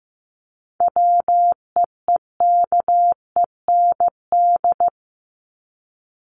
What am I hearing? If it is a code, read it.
Morse code, heard as a keyed tone, about 15 wpm, 700 Hz. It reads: WEEKEND
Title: Top 1751-2000 Words 15wpm